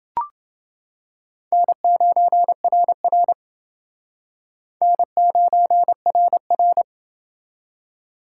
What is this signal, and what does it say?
Morse code, heard as a keyed tone, about 28 wpm, 700 Hz. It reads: N9RR N9RR